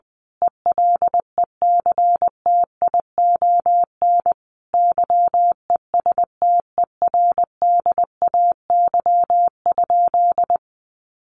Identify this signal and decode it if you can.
Morse code, heard as a keyed tone, about 20 wpm, 700 Hz. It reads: ELECTION YESTERDAY?
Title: A Sound Of Thunder 20wpm